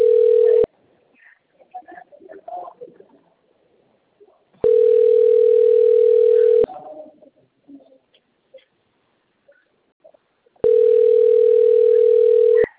Ringback tone (North America), 440 + 480 Hz.